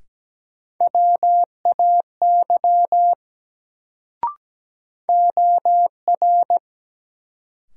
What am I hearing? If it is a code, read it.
Morse code, heard as a keyed tone, about 17 wpm, 700 Hz. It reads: WAY OR
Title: Sets of 2 Words - New Word Drill - 17wpm